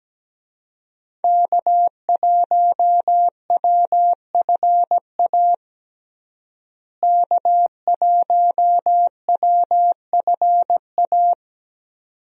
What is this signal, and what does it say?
Morse code, heard as a keyed tone, about 17 wpm, 700 Hz. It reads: K1WFA K1WFA
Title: USA Call Signs - 5 Characters - Repeated 2x - 17wpm